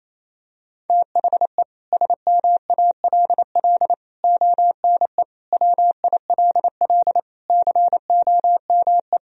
Morse code, heard as a keyed tone, about 27 wpm, 700 Hz. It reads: THE SMALL ONE WILL COME
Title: Sentences from Top 1000 Words - Review Part 1 - 28wpm